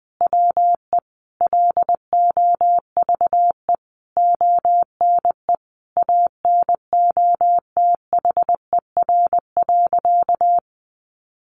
Morse code, heard as a keyed tone, about 20 wpm, 700 Hz. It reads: WE LOVE ONE ANOTHER.
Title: Sentences from Top 600 Words 20wpm